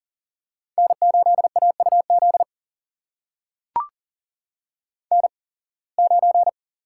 Morse code, heard as a keyed tone, about 39 wpm, spaced as for 26 wpm, 700 Hz. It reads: N8AUZ N9
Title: USA Call Signs - 5 Characters - 8x Character Spacing - 40wpm